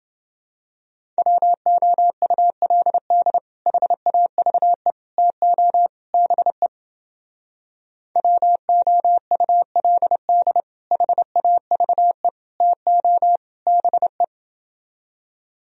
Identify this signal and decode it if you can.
Morse code, heard as a keyed tone, about 29 wpm, 700 Hz. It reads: WOULD HAVE TO BE WOULD HAVE TO BE